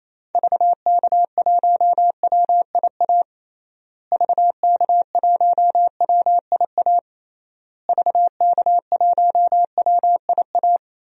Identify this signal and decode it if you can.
Morse code, heard as a keyed tone, about 28 wpm, 700 Hz. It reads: VK1WIA VK1WIA VK1WIA
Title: DXCC-100 Call Signs - 6 Characters - Repeated 3x - 28wpm